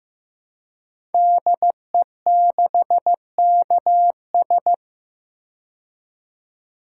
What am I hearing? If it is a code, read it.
Morse code, heard as a keyed tone, about 15 wpm, 700 Hz. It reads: DE6KS
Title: Code Groups - 5 Letter:Numbers - 15wpm